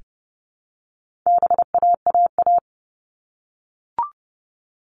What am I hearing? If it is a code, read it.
Morse code, heard as a keyed tone, about 30 wpm, 700 Hz. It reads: BAAA
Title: All Cut Numbers - 4-Digit 30wpm